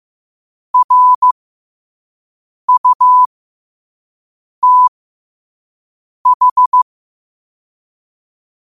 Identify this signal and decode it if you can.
Morse code, heard as a keyed tone, about 15 wpm, spaced as for 5 wpm, 1000 Hz. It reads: RUTH
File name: qotdfiles / 5wpm